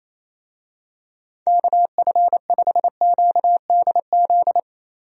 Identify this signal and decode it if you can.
Morse code, heard as a keyed tone, about 28 wpm, 700 Hz. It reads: KF5QDZ
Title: USA Call Signs - 6 Characters - 28wpm